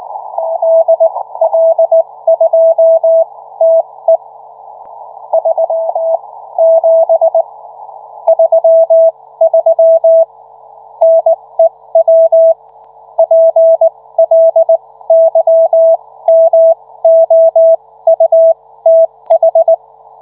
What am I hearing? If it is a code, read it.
Morse code, heard as a keyed tone, about 19 wpm, spaced as for 15 wpm, 660 Hz. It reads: DL2TE 37 33 NEW PLYMOUTH